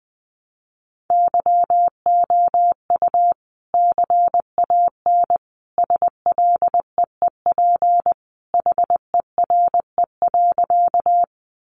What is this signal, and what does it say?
Morse code, heard as a keyed tone, about 20 wpm, 700 Hz. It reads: YOU CAN SLEEP HERE.
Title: Sentences from Top 500 Words 20wpm